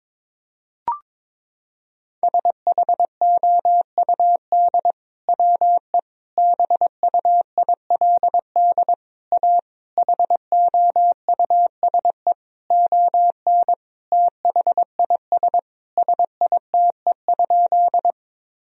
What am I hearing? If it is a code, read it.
Morse code, heard as a keyed tone, about 22 wpm, 700 Hz. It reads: SHOUD WE BUILD A HOUSE ON THIS SITE?
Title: Sentences from Top 1000 Words - Review Part 4 - 22wpm